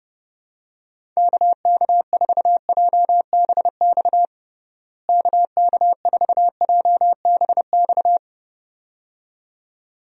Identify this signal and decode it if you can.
Morse code, heard as a keyed tone, about 32 wpm, 700 Hz. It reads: KK4JBX KK4JBX
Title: USA Call Signs - 6 Characters - Repeated 2x - 30wpm